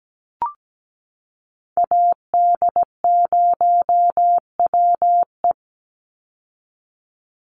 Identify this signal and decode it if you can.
Morse code, heard as a keyed tone, about 17 wpm, 700 Hz. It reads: AD0WE
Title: Call Signs Repeated 3x - 17wpm